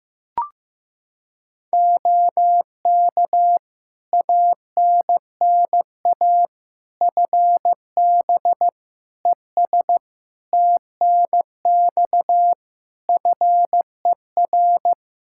Morse code, heard as a keyed tone, about 15 wpm, 700 Hz. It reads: OK ANNA FB ES TNX FER
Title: Practice QSO - Exchange 2 - Info - 15wpm